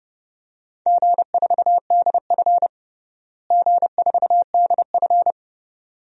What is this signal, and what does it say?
Morse code, heard as a keyed tone, about 30 wpm, 700 Hz. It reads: G4DF G4DF
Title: DXCC-100 Call Signs - 4 Characters - Repeated 2x - 30wpm